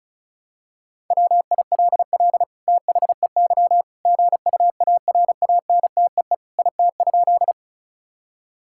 Morse code, heard as a keyed tone, about 34 wpm, 700 Hz. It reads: WILL THEY GUARANTEE IT?
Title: Sentences from Top 1600 Words 35wpm